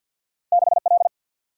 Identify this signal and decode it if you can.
Morse code, heard as a keyed tone, about 50 wpm, 700 Hz. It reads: 6 L
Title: Single-Letter Single-Number 50wpm